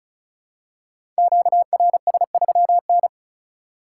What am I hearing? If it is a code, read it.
Morse code, heard as a keyed tone, about 34 wpm, 700 Hz. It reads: QRS3N
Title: Code Groups - 5 Letter:Numbers - 35wpm